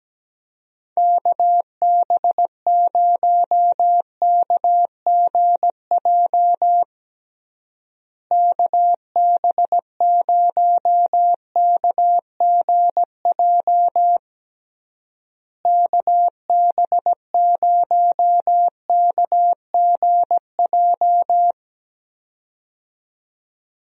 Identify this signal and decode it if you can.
Morse code, heard as a keyed tone, about 17 wpm, 700 Hz. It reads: KB0KGJ KB0KGJ KB0KGJ
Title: USA Call Signs - 6 Characters - Repeated 3x - 17wpm